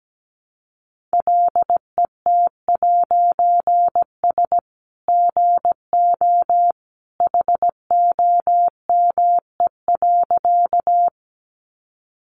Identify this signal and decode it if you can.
Morse code, heard as a keyed tone, about 17 wpm, 700 Hz. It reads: LET'S GO HOME.
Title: A Sound Of Thunder 17wpm